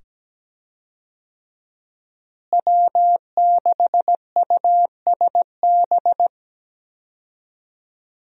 Morse code, heard as a keyed tone, about 17 wpm, 700 Hz. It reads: W6USB
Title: USA Call Signs - 5 Characters - Repeated 2x - 17wpm